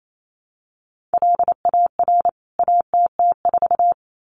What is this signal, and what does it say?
Morse code, heard as a keyed tone, about 28 wpm, 700 Hz. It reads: LAR ATT4
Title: CWT Exchange 28wpm